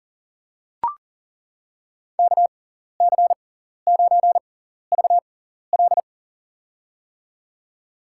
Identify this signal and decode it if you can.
Morse code, heard as a keyed tone, about 40 wpm, spaced as for 14 wpm, 700 Hz. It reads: KC9VL
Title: USA Call Signs - 6x Character Spacing - 40wpm